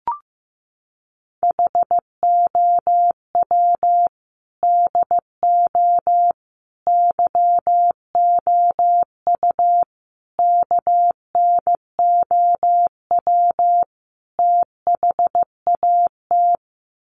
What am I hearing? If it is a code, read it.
Morse code, heard as a keyed tone, about 15 wpm, 700 Hz. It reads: HOW DO YOU KNOW THAT
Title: Sets of 5 Words Encore 15wpm